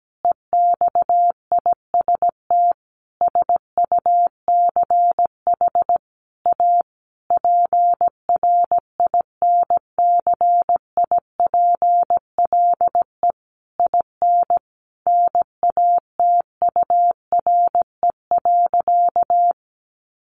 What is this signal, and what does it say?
Morse code, heard as a keyed tone, about 17 wpm, 700 Hz. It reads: EXIST SUCH A PRINCIPLE IN NATURE.
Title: Sentences from Top 1400 Words 17wpm